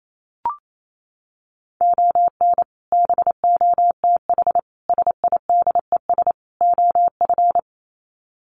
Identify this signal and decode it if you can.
Morse code, heard as a keyed tone, about 28 wpm, 700 Hz. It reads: ON BOTH SIDES OF
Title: Sets of 4 Words 28wpm